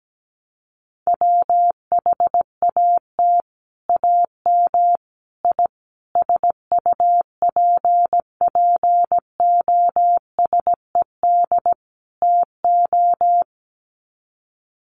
Morse code, heard as a keyed tone, about 17 wpm, 700 Hz. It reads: WHAT AM I SUPPOSED TO
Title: Sets of 5 Words Encore2 17wpm